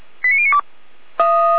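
Five-tone selective call (ZVEI), heard as quick runs of five tones, tones of about 70 ms. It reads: 89092
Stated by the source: Five-tone ZVEI selective call received on SDR.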